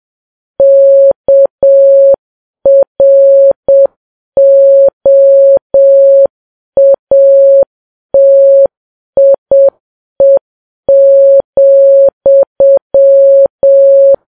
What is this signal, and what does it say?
Morse code, heard as a keyed tone, about 7 wpm, 550 Hz. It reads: KROATIE,
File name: Files Web tls / Morse on9fv / morse7kps